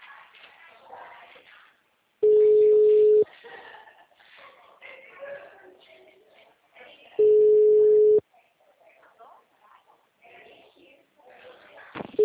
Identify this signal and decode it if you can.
Ringback tone (Europe, 425 Hz), with speech.
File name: out- -2004- - - .18411